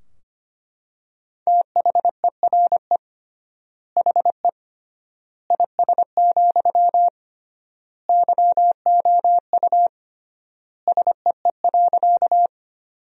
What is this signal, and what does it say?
Morse code, heard as a keyed tone, about 25 wpm, 700 Hz. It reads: THERE HE IS, YOU SEE.
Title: 3x Word-Spacing - Sentences from Top 100 Words - 25wpm